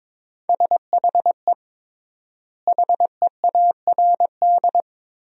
Morse code, heard as a keyed tone, about 22 wpm, 700 Hz. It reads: SHE HEARD
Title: 3x Word-Spacing - Sentences from Top 500 Words - 22wpm